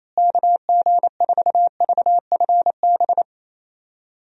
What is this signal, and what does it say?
Morse code, heard as a keyed tone, about 28 wpm, 700 Hz. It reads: KG4VFB
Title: USA Call Signs - 6 Characters - Repeated 2x - 28wpm